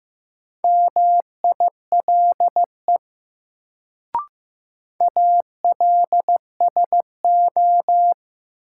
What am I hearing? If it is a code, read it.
Morse code, heard as a keyed tone, about 15 wpm, 700 Hz. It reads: MILE ALSO
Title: Sentences from Top 400 Words - New Word Drill - 15wpm